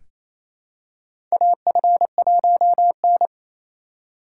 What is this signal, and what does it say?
Morse code, heard as a keyed tone, about 28 wpm, 700 Hz. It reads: AF1N
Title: USA Call Signs - 4 Characters - 2x Character Spacing - 28wpm